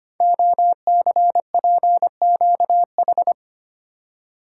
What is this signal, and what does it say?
Morse code, heard as a keyed tone, about 25 wpm, 700 Hz. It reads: OCPQH
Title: Code Groups - 5 Letter:Numbers - 25wpm